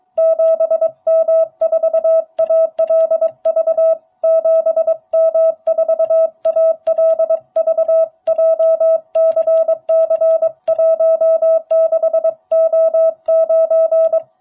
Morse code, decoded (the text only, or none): 7M4ALV 7M4ALV JCC16O9